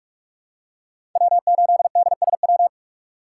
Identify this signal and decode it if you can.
Morse code, heard as a keyed tone, about 46 wpm, 700 Hz. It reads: W8DSW
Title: USA Call Signs - 2x Character Spacing - 45wpm